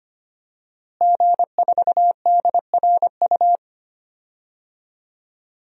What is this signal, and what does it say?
Morse code, heard as a keyed tone, about 25 wpm, 700 Hz. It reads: G4DRU
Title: DXCC-100 Call Signs - 5 Characters - Repeated 3x - 25wpm